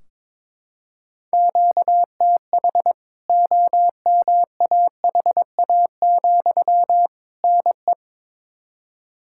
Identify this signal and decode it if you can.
Morse code, heard as a keyed tone, about 22 wpm, 700 Hz. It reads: QTH OMAHA, NE
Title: K1USN SST QSO Practice 22wpm